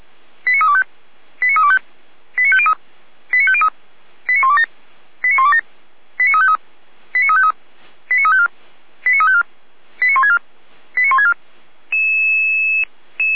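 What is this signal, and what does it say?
Five-tone selective call (ZVEI), heard as quick runs of five tones, tones of about 70 ms. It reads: 89325 89325 89503 89503 89127 89127 89353 89353 89354 89354 89164 89164
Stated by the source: ZVEI five-tone call received by SDR.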